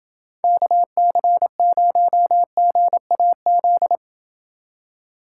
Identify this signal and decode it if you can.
Morse code, heard as a keyed tone, about 29 wpm, 700 Hz. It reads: KC0GAZ